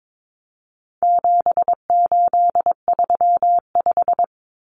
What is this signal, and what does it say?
Morse code, heard as a keyed tone, about 22 wpm, 700 Hz. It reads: 7835
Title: Numbers - 4-Digit 22wpm